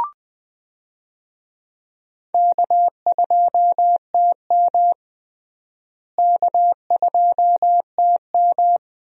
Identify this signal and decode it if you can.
Morse code, heard as a keyed tone, about 21 wpm, 700 Hz. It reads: K2TM K2TM
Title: USA Call Signs - 4 Characters - Repeated 2x - 20wpm